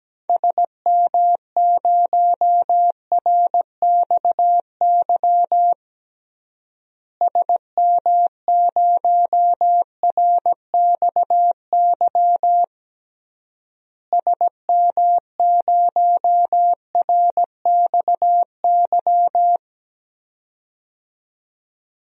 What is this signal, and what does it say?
Morse code, heard as a keyed tone, about 17 wpm, 700 Hz. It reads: SM0RXY SM0RXY SM0RXY
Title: DXCC-100 Call Signs - 6 Characters - Repeated 3x - 17wpm